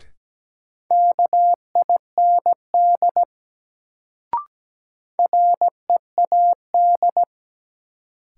Morse code, heard as a keyed tone, about 17 wpm, 700 Hz. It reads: KIND READ
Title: Sentences from Top 200 Words - New Word Drill - 17wpm